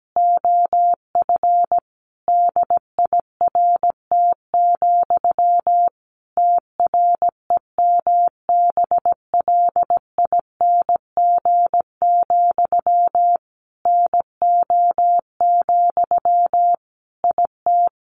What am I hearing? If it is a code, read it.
Morse code, heard as a keyed tone, about 17 wpm, 700 Hz. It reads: OF DIRT, TREMBLING, NO, IT
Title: A Sound Of Thunder 17wpm